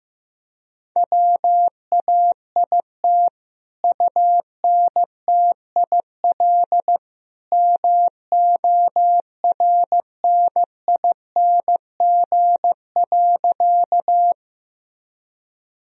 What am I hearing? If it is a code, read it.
Morse code, heard as a keyed tone, about 15 wpm, 700 Hz. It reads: WAIT UNTIL MORNING.
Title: Sentences from Top 1000 Words - Review Part 1 - 15wpm